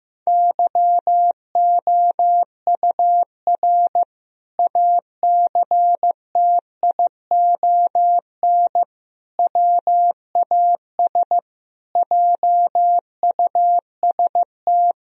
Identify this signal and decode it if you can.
Morse code, heard as a keyed tone, about 15 wpm, 700 Hz. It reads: YOUR ACTION WAS JUST
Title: Sentences from Top 1000 Words - Review Part 4 - 15wpm